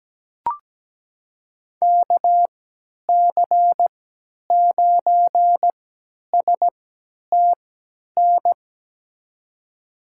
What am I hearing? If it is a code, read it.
Morse code, heard as a keyed tone, about 17 wpm, spaced as for 10 wpm, 700 Hz. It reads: KC9STN
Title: USA Call Signs - 3x Character Spacing - 17wpm